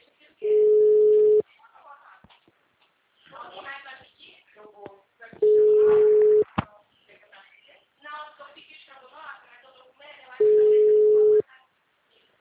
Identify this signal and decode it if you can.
Ringback tone (Europe, 425 Hz), with speech.
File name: out- -2032- - - .37